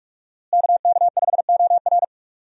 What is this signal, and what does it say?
Morse code, heard as a keyed tone, about 46 wpm, 700 Hz. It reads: KK5OR
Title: USA Call Signs - 5 Characters - Repeated 3x - 45wpm